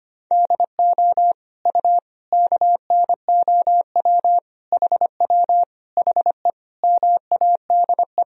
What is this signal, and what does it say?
Morse code, heard as a keyed tone, about 25 wpm, 700 Hz. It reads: DO U KNOW HW HE MADE
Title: Sentences from Top 300 Words and Abbreviations 25wpm